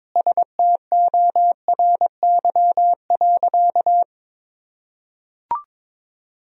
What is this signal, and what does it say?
Morse code, heard as a keyed tone, about 21 wpm, 700 Hz. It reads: STORY.
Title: Sentences from Top 1000 Words - Review Part 1 - 22wpm